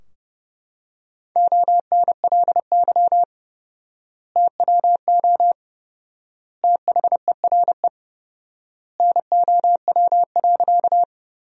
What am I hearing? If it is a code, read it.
Morse code, heard as a keyed tone, about 30 wpm, 700 Hz. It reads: ONLY TWO THERE NOW.
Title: 4x Word-Spacing - Sentences from Top 200 Words - 30wpm